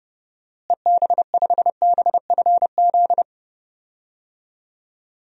Morse code, heard as a keyed tone, about 32 wpm, 700 Hz. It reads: EB5BFZ
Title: DXCC-100 Call Signs - 6 Characters - Repeated 2x - 30wpm